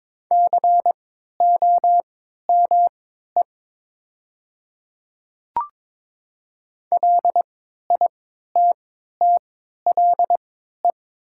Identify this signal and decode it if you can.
Morse code, heard as a keyed tone, about 21 wpm, spaced as for 12 wpm, 700 Hz. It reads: COME LITTLE
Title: 3x Character Spacing - Top 200 Words - 22wpm